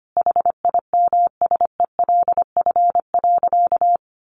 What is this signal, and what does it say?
Morse code, heard as a keyed tone, about 25 wpm, 700 Hz. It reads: HIMSELF.
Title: Additional Easy Phrases 25wpm